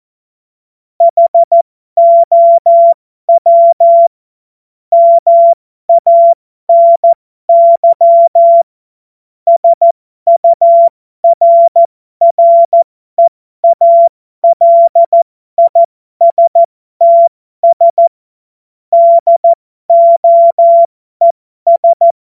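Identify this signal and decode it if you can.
Morse code, heard as a keyed tone, about 14 wpm, spaced as for 12 wpm, 680 Hz. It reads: HOW MANY SURREALISTS DOES
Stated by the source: Morse code joke practice.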